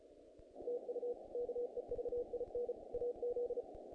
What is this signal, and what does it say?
Morse code, heard as a keyed tone, about 35 wpm, 470 Hz. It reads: TU KEVIN AZ
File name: contests / naqp-cw-k5tr-op-k5pi- -1900